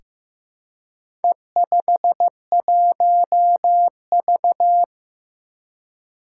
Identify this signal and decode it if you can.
Morse code, heard as a keyed tone, about 15 wpm, 700 Hz. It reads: E51V
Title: Four Challenging Letter-Number Combinations 15wpm